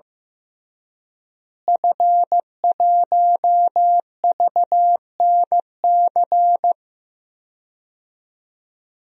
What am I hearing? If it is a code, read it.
Morse code, heard as a keyed tone, about 15 wpm, 700 Hz. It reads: F1VNC
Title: DXCC-100 Call Signs - 5 Characters - Repeated 3x - 15wpm